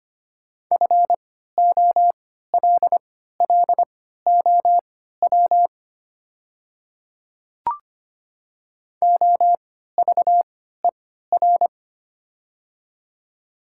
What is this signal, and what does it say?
Morse code, heard as a keyed tone, about 25 wpm, spaced as for 14 wpm, 700 Hz. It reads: FOLLOW OVER
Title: 3x Character Spacing - Top 300 Words - 25wpm